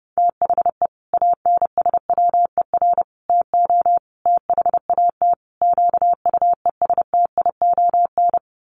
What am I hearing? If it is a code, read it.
Morse code, heard as a keyed tone, about 28 wpm, 700 Hz. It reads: THE ANSWER TO THAT QUESTION
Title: Sets of 5 Words Encore2 30wpm